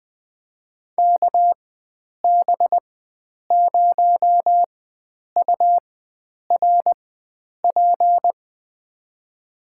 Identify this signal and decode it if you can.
Morse code, heard as a keyed tone, about 20 wpm, spaced as for 9 wpm, 700 Hz. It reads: KB0URP
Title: USA Call Signs - 6 Characters - 4x Character Spacing - 20wpm